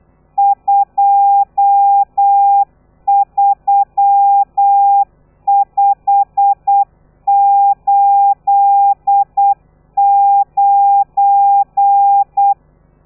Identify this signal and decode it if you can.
Morse code, heard as a keyed tone, about 8 wpm, 800 Hz. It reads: 23589
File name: telegrafia / i2tfj / Lez12 35